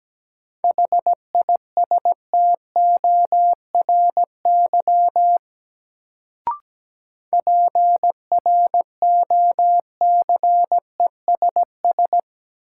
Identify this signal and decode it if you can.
Morse code, heard as a keyed tone, about 17 wpm, 700 Hz. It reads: HISTORY PROCESS
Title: Sets of 3 Words - New Word Drill - 17wpm